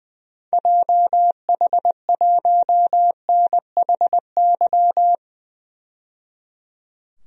Morse code, heard as a keyed tone, about 21 wpm, 700 Hz. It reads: JH1NHY
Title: DXCC-100 Call Signs - 6 Characters - 20wpm